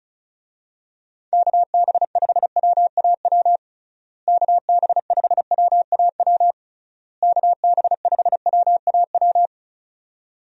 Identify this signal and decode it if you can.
Morse code, heard as a keyed tone, about 36 wpm, 700 Hz. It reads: KB5WAW KB5WAW KB5WAW
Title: USA Call Signs - 6 Characters - Repeated 3x - 35wpm